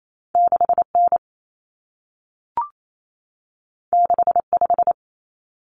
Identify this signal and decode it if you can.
Morse code, heard as a keyed tone, about 28 wpm, 700 Hz. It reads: 6N 65
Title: Common Cut Numbers - 2-Digit 28wpm